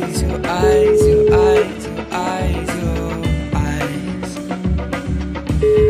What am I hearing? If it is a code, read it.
Ringback tone (Europe, 425 Hz), with noise.